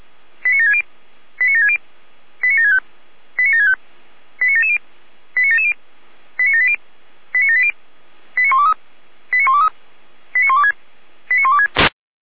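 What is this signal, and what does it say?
Five-tone selective call (ZVEI), heard as quick runs of five tones, tones of about 70 ms. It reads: 89760 89760 89765 89765 89770 89770 89780 89780 89123 89123 89126 89126
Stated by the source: ZVEI five-tone selective call received by SDR.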